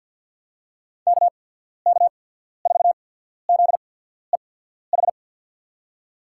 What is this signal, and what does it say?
Morse code, heard as a keyed tone, about 48 wpm, spaced as for 14 wpm, 690 Hz. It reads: KK4ZEH
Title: USA Call Signs - 6 Characters - 8x Character Spacing - 50wpm